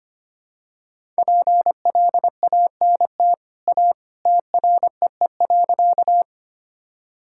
Morse code, heard as a keyed tone, about 25 wpm, 700 Hz. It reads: PLANT A TREE.
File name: Easy Sentences with Hints - 25wpm